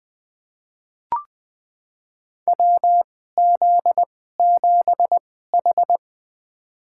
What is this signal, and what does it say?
Morse code, heard as a keyed tone, about 20 wpm, spaced as for 14 wpm, 700 Hz. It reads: WZ7H
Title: USA Call Signs - 2x Character Spacing - 20wpm